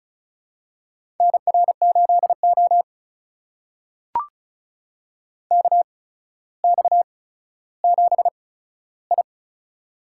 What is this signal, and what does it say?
Morse code, heard as a keyed tone, about 35 wpm, spaced as for 15 wpm, 700 Hz. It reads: NR8O KX7I
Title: USA Call Signs - 4 Characters - 8x Character Spacing - 35wpm